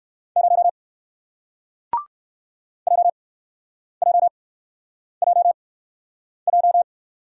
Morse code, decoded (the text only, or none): 0 JJJJ